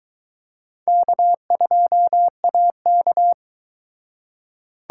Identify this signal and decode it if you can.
Morse code, heard as a keyed tone, about 24 wpm, 700 Hz. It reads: K2AK